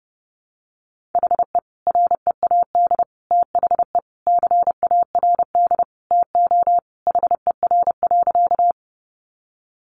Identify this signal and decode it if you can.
Morse code, heard as a keyed tone, about 30 wpm, 700 Hz. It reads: HE READ THE CARD TO HER.
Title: Sentences from Top 600 Words 30wpm